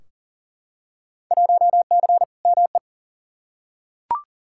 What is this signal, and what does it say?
Morse code, heard as a keyed tone, about 40 wpm, 700 Hz. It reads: 1C ME
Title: Field Day Exchanges - 40wpm